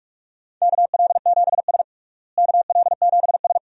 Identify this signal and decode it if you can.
Morse code, heard as a keyed tone, about 46 wpm, 700 Hz. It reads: KL7S KL7S
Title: DXCC-100 Call Signs - 4 Characters - Repeated 2x - 45wpm